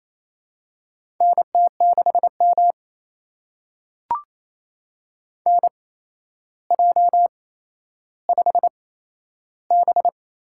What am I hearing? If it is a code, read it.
Morse code, heard as a keyed tone, about 28 wpm, spaced as for 12 wpm, 700 Hz. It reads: NT6M NJ5B
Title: USA Call Signs - 4 Characters - 8x Character Spacing - 28wpm